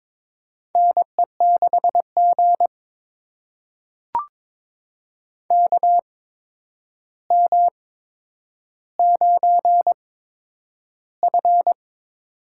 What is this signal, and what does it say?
Morse code, heard as a keyed tone, about 22 wpm, spaced as for 9 wpm, 700 Hz. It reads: NE6G KM9F
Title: USA Call Signs - 4 Characters - 8x Character Spacing - 22wpm